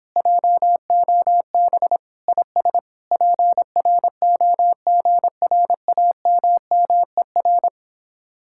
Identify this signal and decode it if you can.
Morse code, heard as a keyed tone, about 29 wpm, 700 Hz. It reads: JOB IS PROGRAMMER